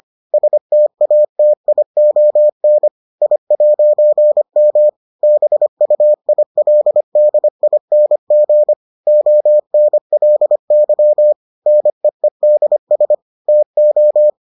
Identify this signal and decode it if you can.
Morse code, heard as a keyed tone, about 25 wpm, 600 Hz. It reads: STATION I'M BUILDING ONLY NEEDS TO